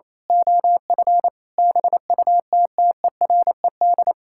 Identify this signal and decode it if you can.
Morse code, heard as a keyed tone, about 27 wpm, 700 Hz. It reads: OF BUTTERED
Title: Pangram Challenge v2 - 28wpm